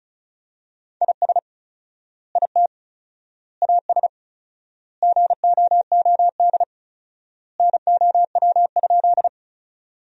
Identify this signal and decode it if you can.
Morse code, heard as a keyed tone, about 34 wpm, 700 Hz. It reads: IS IT AS GOOD NOW?
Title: 4x Word-Spacing - Sentences from Top 200 Words - 35wpm